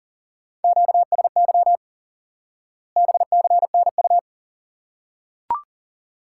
Morse code, heard as a keyed tone, about 40 wpm, 700 Hz. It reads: QSY BCNU
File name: course / instant-qso-element-course-3 / IQE Course 3 - Lesson 89 - YAGI - Two Elements - 40wpm